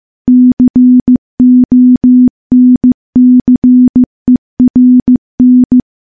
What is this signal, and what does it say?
Morse code, heard as a keyed tone, about 15 wpm, 250 Hz. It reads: CONCERN
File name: morse-training-15WPM